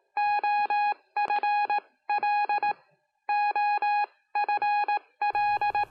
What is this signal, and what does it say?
Morse code, heard as a keyed tone, about 18 wpm, spaced as for 16 wpm, 830 Hz. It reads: OFL OFL